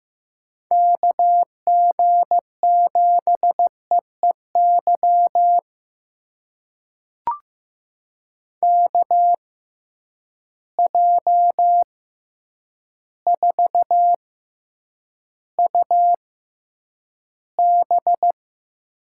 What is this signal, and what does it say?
Morse code, heard as a keyed tone, about 15 wpm, spaced as for 8 wpm, 700 Hz. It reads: KG7EEY KJ4UB
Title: USA Call Signs - 6 Characters - 6x Character Spacing - 15wpm